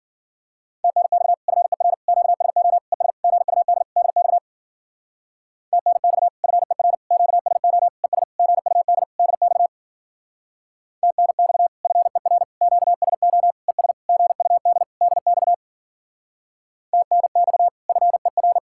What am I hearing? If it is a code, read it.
Morse code, heard as a keyed tone, about 50 wpm, 700 Hz. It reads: TNX FER QSO ES GUD DX TNX FER QSO ES GUD DX TNX FER QSO ES GUD DX TNX FER